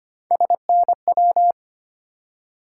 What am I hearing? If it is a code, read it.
Morse code, heard as a keyed tone, about 25 wpm, 700 Hz. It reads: SNW
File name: Three Letter-Number Combinations - 25wpm